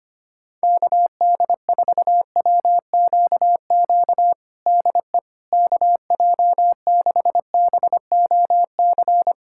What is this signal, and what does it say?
Morse code, heard as a keyed tone, about 25 wpm, 700 Hz. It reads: KD4WQQ DE KJ6BOC
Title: USA Callsign DE Callsign - 6 Characters - 25wpm